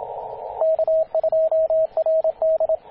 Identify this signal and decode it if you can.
Morse code, heard as a keyed tone, about 27 wpm, 640 Hz, with radio noise. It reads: K2RD
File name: t8cw-18-0737mhz-cw